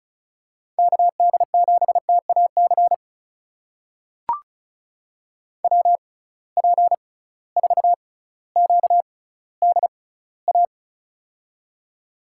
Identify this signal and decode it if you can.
Morse code, heard as a keyed tone, about 36 wpm, spaced as for 18 wpm, 700 Hz. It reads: KD7TAC WP4QDA
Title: USA Call Signs - 6 Characters - 6x Character Spacing - 35wpm